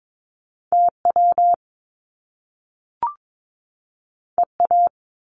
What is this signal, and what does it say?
Morse code, heard as a keyed tone, about 22 wpm, 700 Hz. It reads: TW EA
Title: Top Two Letter Combinations in Words 22wpm